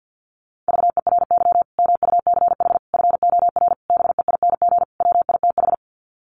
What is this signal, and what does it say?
Morse code, heard as a keyed tone, about 50 wpm, 700 Hz. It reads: VERY MUCH FOR BEING WITH
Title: Sets of 5 Words 50wpm